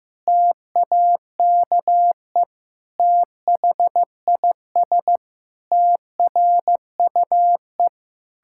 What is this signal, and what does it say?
Morse code, heard as a keyed tone, about 15 wpm, 700 Hz. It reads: TAKE THIS TRUE
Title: Pangram Challenge v2 - 15wpm